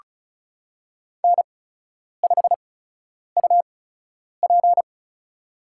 Morse code, heard as a keyed tone, about 36 wpm, spaced as for 10 wpm, 700 Hz. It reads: N5UP
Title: USA Call Signs - 4 Characters - 8x Character Spacing - 35wpm